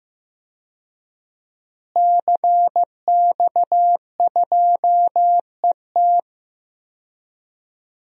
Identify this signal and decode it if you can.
Morse code, heard as a keyed tone, about 15 wpm, 700 Hz. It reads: CX2ET
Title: DXCC-100 Call Signs - 5 Characters - Repeated 3x - 15wpm